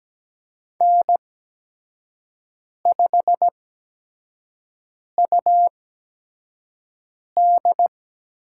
Morse code, heard as a keyed tone, about 17 wpm, spaced as for 5 wpm, 700 Hz. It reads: N5UD
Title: USA Call Signs - 5 Characters - 8x Character Spacing - 17wpm